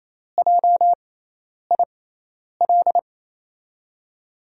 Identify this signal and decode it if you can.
Morse code, heard as a keyed tone, about 28 wpm, spaced as for 10 wpm, 700 Hz. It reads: JIL
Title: USA Call Signs - 6 Characters - 6x Character Spacing - 28wpm